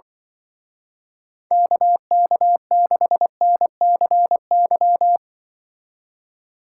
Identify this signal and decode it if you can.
Morse code, heard as a keyed tone, about 24 wpm, 700 Hz. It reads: KK6NCY
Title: Speed Racing - USA 2x3 Call Signs 20wpm